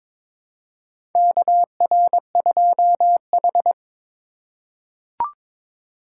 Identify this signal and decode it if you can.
Morse code, heard as a keyed tone, about 21 wpm, 700 Hz. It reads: KR2H
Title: USA Call Signs - 4 Characters - 3x Character Spacing - 22wpm